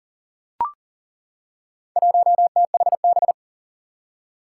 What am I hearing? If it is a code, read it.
Morse code, heard as a keyed tone, about 40 wpm, 700 Hz. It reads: 1THB
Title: Four Challenging Letter-Number Combinations 40wpm